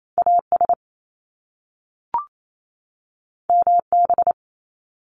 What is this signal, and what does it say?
Morse code, heard as a keyed tone, about 28 wpm, 700 Hz. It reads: AS MB
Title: Top Two Letter Combinations in Words 28wpm